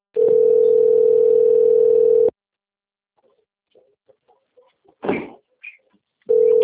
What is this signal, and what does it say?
Ringback tone (North America), 440 + 480 Hz.